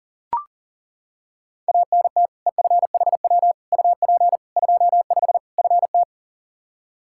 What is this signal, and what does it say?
Morse code, heard as a keyed tone, about 40 wpm, 700 Hz. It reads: ANT EFHW UP 25 FT
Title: K1USN SST QSO Practice 40wpm